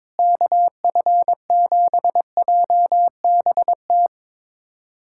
Morse code, heard as a keyed tone, about 22 wpm, 700 Hz. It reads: KF7JBT
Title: USA Callsign DE Callsign 22wpm